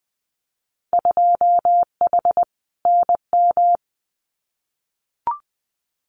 Morse code, heard as a keyed tone, about 20 wpm, 700 Hz. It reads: 2H NM
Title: Winter Field Day Exchanges Encore 20wpm